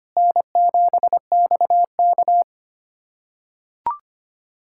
Morse code, heard as a keyed tone, about 25 wpm, 700 Hz. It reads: N7XK
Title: USA Callsign DE Callsign - 4 Characters - 25wpm